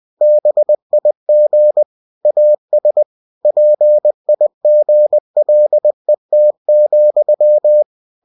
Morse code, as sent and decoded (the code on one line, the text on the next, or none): -... .. --. / .- ... / .--. .. --. .-.. - --..--
BIG AS PIGLET,